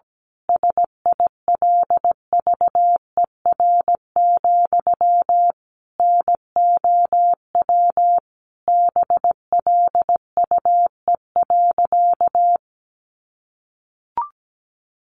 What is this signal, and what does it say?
Morse code, heard as a keyed tone, about 17 wpm, 700 Hz. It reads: SILVER, NOW BLUE.
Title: A Sound Of Thunder 17wpm